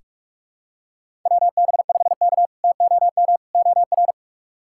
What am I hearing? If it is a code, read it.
Morse code, heard as a keyed tone, about 43 wpm, 700 Hz. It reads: WB5K TOM OR
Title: K1USN SST QSO Practice - 45wpm